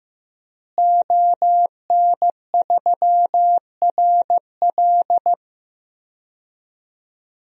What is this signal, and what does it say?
Morse code, heard as a keyed tone, about 15 wpm, 700 Hz. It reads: ON3RL
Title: DXCC-100 Call Signs - 5 Characters - Repeated 3x - 15wpm